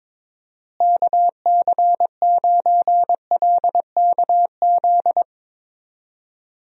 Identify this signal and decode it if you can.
Morse code, heard as a keyed tone, about 21 wpm, 700 Hz. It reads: KC9LKZ
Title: USA Call Signs - 6 Characters - Repeated 2x - 22wpm